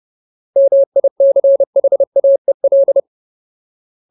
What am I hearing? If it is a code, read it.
Morse code, heard as a keyed tone, about 30 wpm, 550 Hz. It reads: MICHAEL
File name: Ultimate Word List / 30WPM Ultimate